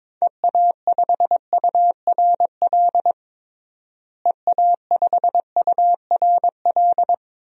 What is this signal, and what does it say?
Morse code, heard as a keyed tone, about 22 wpm, 700 Hz. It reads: EA5URL EA5URL
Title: DXCC-100 Call Signs - 6 Characters - Repeated 2x - 22wpm